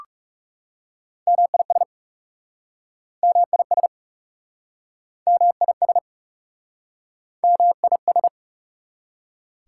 Morse code, heard as a keyed tone, about 40 wpm, 700 Hz. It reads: MIS MIS MIS MIS